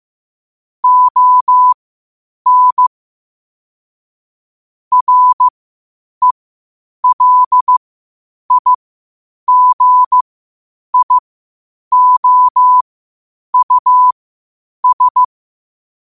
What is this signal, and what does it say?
Morse code, heard as a keyed tone, about 15 wpm, spaced as for 8 wpm, 1000 Hz. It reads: ON RELIGIOUS
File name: qotdfiles / 8wpm